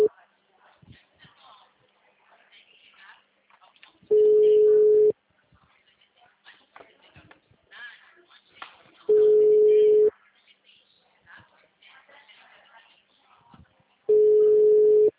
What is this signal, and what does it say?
Ringback tone (Europe, 425 Hz), with speech.
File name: out- -2012- - - .133